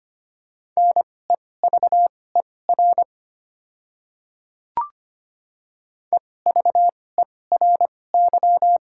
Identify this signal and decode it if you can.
Morse code, heard as a keyed tone, about 25 wpm, spaced as for 18 wpm, 690 Hz. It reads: NEVER EVERY
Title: 2x Character Spacing - Top 300 Words - 25wpm